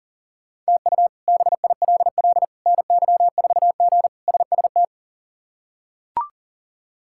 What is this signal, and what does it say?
Morse code, heard as a keyed tone, about 40 wpm, 700 Hz. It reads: TU BILL NY4G SST
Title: K1USN SST QSO Practice - 40wpm